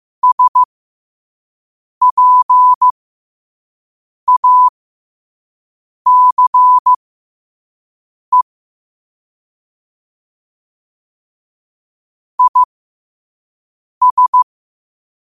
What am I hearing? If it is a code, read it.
Morse code, heard as a keyed tone, about 15 wpm, spaced as for 5 wpm, 1000 Hz. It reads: SPACE IS
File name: qotdfiles / 5wpm